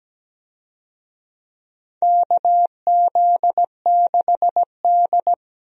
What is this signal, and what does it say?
Morse code, heard as a keyed tone, about 17 wpm, 700 Hz. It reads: KZ6D
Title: USA Call Signs - 4 Characters - 17wpm